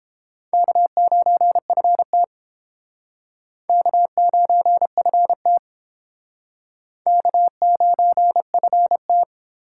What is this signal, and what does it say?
Morse code, heard as a keyed tone, about 30 wpm, 700 Hz. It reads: K9FT K9FT K9FT